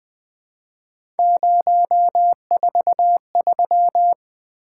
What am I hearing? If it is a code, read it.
Morse code, heard as a keyed tone, about 20 wpm, 700 Hz. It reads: 043
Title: Three Letter-Number Combinations - 20wpm